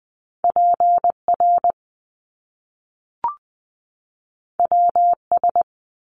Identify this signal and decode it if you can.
Morse code, heard as a keyed tone, about 20 wpm, 700 Hz. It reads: PR WS
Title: Top Two Letter Combinations in Words 20wpm